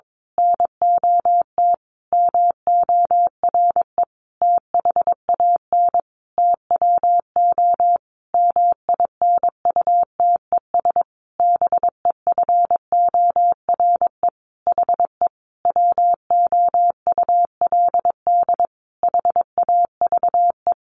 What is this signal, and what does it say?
Morse code, heard as a keyed tone, about 22 wpm, 700 Hz. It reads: NOT MORE THAN TWO MINUTES BEFORE HE WOULD HAVE
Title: A Sound Of Thunder 22wpm